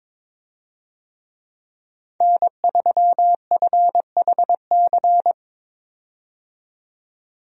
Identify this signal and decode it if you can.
Morse code, heard as a keyed tone, about 22 wpm, 700 Hz. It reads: N3FHC
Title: USA Call Signs - 5 Characters - 22wpm